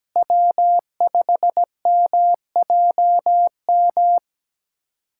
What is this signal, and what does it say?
Morse code, heard as a keyed tone, about 17 wpm, 700 Hz. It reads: W5MJM
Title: USA Callsign DE Callsign - 5 Characters - 17wpm